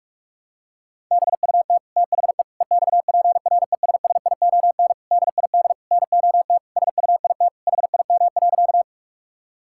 Morse code, heard as a keyed tone, about 41 wpm, 700 Hz. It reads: BUT THE EXPRESSION DID NOT SUIT HIM.
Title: Sentences from Top 1600 Words 45wpm